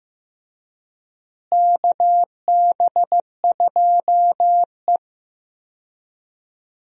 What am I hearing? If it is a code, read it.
Morse code, heard as a keyed tone, about 15 wpm, 700 Hz. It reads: KB2E
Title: USA Call Signs - 4 Characters - Repeated 3x - 15wpm